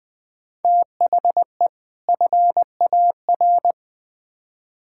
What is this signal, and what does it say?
Morse code, heard as a keyed tone, about 20 wpm, 700 Hz. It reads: THE FAR
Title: Build A Phrase - 5-Word Phrases - 20wpm